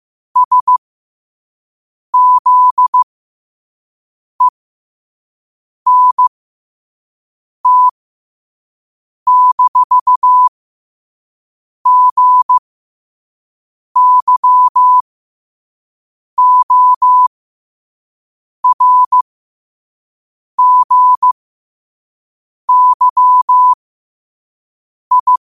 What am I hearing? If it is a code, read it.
Morse code, heard as a keyed tone, about 15 wpm, spaced as for 5 wpm, 1000 Hz. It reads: SZENT-GYORGYI
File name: 5wpm